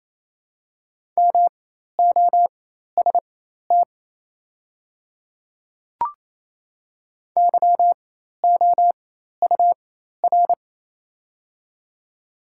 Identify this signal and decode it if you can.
Morse code, heard as a keyed tone, about 28 wpm, spaced as for 13 wpm, 690 Hz. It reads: MOST YOUR
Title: 4x Character Spacing - Top 100 Words - 28wpm